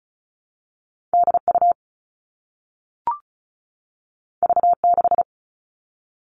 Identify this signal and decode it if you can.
Morse code, heard as a keyed tone, about 36 wpm, 700 Hz. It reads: DU V6
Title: All Cut Numbers - 2-Digit 35wpm